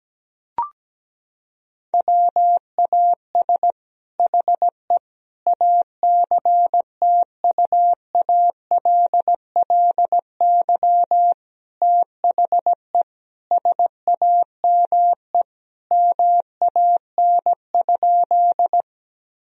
Morse code, heard as a keyed tone, about 17 wpm, 700 Hz. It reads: WAS HE ACTUALLY THE SAME MAN?
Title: Sentences from Top 1000 Words - Review Part 4 - 17wpm